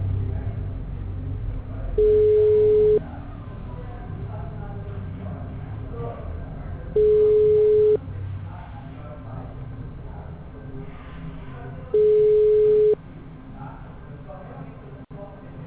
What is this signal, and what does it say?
Ringback tone (Europe, 425 Hz), with speech.
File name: out- -2004- - - .27636